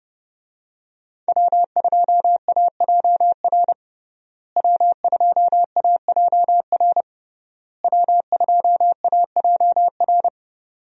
Morse code, heard as a keyed tone, about 30 wpm, 700 Hz. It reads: W2AJR W2AJR W2AJR
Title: USA Call Signs - 5 Characters - Repeated 3x - 30wpm